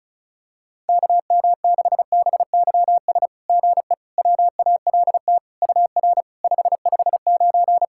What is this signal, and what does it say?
Morse code, heard as a keyed tone, about 34 wpm, 700 Hz. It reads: KM6BYS GE WALT UR 559
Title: SOTA QSO Practice 35wpm